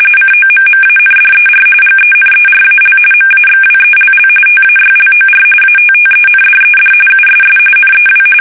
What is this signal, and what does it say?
RTTY, about 45.45 baud. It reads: COMMANDER GENERAL MAXWELL TAYLOR ATTENDED USS NEW
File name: fsk850 / battleshipnj